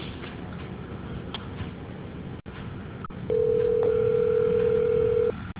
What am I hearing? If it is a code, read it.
Ringback tone (North America), 440 + 480 Hz.